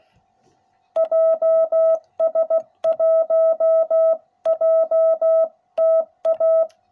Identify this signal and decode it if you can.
Morse code, heard as a keyed tone, about 16 wpm, 660 Hz. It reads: JS1JTA